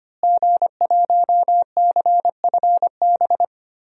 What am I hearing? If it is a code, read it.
Morse code, heard as a keyed tone, about 25 wpm, 700 Hz. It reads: G1CFB
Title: DXCC-100 Call Signs - 5 Characters - Repeated 2x - 25wpm